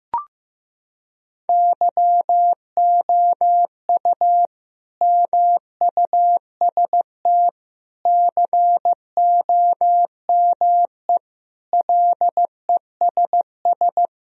Morse code, heard as a keyed tone, about 16 wpm, 700 Hz. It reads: YOU MUST COME LESS
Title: Sentences from Top 1000 Words - Review Part 2 - 15wpm